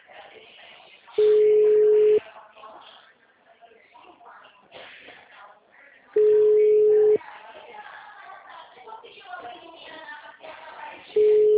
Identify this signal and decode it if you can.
Ringback tone (Europe, 425 Hz), with speech.
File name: out- -2054- - - .167